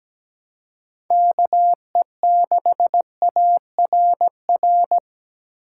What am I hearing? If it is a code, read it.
Morse code, heard as a keyed tone, about 17 wpm, 700 Hz. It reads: KE6ARR
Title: USA Call Signs - 6 Characters - 2x Character Spacing - 17wpm